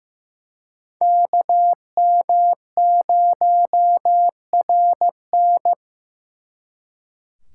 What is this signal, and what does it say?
Morse code, heard as a keyed tone, about 18 wpm, 700 Hz. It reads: KM0RN
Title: Speed Racing - USA 2x2 Call Signs 15wpm